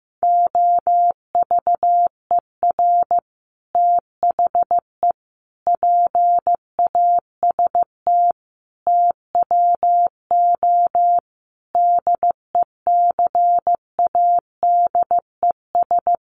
Morse code, heard as a keyed tone, about 15 wpm, 700 Hz. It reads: OVER THE PAST TWO DECADES
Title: Sets of 5 Words Encore2 15wpm